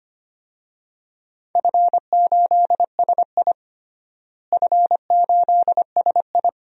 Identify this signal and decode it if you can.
Morse code, heard as a keyed tone, about 25 wpm, 700 Hz. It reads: F8SI F8SI
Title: DXCC-100 Call Signs - 4 Characters - Repeated 2x - 25wpm